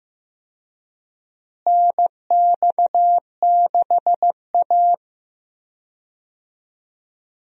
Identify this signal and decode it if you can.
Morse code, heard as a keyed tone, about 15 wpm, 700 Hz. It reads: NX6A
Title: USA Call Signs - 4 Characters - Repeated 3x - 15wpm